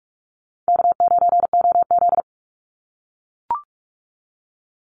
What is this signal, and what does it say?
Morse code, heard as a keyed tone, about 44 wpm, 700 Hz. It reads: K9OZ
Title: NA SOTA Chaser Callsigns 45wpm